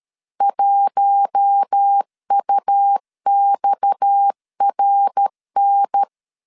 Morse code, heard as a keyed tone, about 13 wpm, 800 Hz. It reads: 1UXRN
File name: LEZIONI / 6 PKX0 / 6 PKX 12wpm 2024